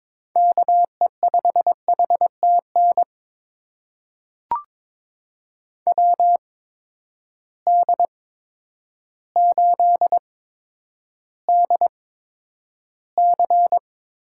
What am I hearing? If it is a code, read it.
Morse code, heard as a keyed tone, about 22 wpm, spaced as for 10 wpm, 700 Hz. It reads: KE5HTN WD8DC
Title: USA Call Signs - 6 Characters - 8x Character Spacing - 22wpm